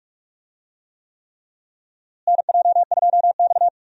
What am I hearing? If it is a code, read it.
Morse code, heard as a keyed tone, about 42 wpm, 700 Hz. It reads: NJ2X
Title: USA Call Signs - 4 Characters - Repeated 3x - 45wpm